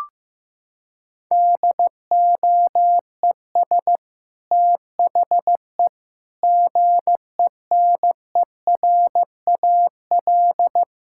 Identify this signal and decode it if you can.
Morse code, heard as a keyed tone, about 16 wpm, 700 Hz. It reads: DOES THE GENERAL
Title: Sentences from Top 1000 Words - Review Part 6 - 15wpm